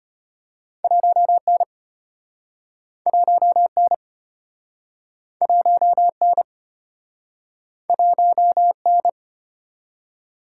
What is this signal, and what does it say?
Morse code, heard as a keyed tone, about 34 wpm, 700 Hz. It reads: 1N 1N 1N 1N